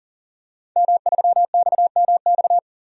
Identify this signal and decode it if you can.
Morse code, heard as a keyed tone, about 40 wpm, 700 Hz. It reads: M3XMX
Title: DXCC-100 Call Signs - 5 Characters - 40wpm